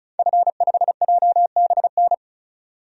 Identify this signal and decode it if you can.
Morse code, heard as a keyed tone, about 34 wpm, 700 Hz. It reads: F5JBN
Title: DXCC-100 Call Signs - 5 Characters - 35wpm